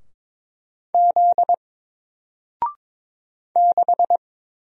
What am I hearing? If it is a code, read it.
Morse code, heard as a keyed tone, about 22 wpm, 700 Hz. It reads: Z 6
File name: Single-Letter-Number-22wpm